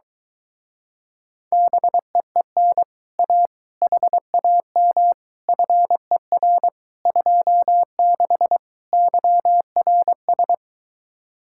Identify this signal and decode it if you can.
Morse code, heard as a keyed tone, about 23 wpm, 700 Hz. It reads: BEEN A HAM FER 26 YRS